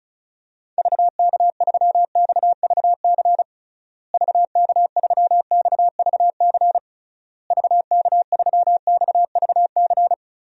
Morse code, heard as a keyed tone, about 34 wpm, 700 Hz. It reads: VK3XVC VK3XVC VK3XVC
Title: DXCC-100 Call Signs - 6 Characters - Repeated 3x - 35wpm